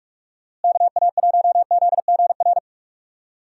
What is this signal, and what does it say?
Morse code, heard as a keyed tone, about 46 wpm, 700 Hz. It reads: KA1ZGR
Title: USA Call Signs - 6 Characters - Repeated 3x - 45wpm